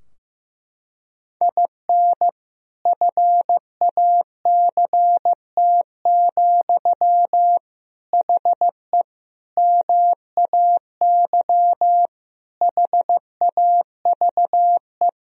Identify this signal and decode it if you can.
Morse code, heard as a keyed tone, about 15 wpm, 700 Hz. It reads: IN FACT, HE MAY HAVE
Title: Sentences from Top 1000 Words - Review Part 3 - 15wpm